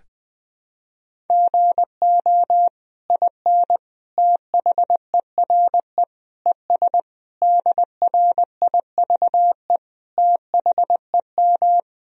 Morse code, heard as a keyed tone, about 20 wpm, 700 Hz. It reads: GO IN THERE ES DRIVE THEM
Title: Sentences from Top 500 Words and Abbreviations 20wpm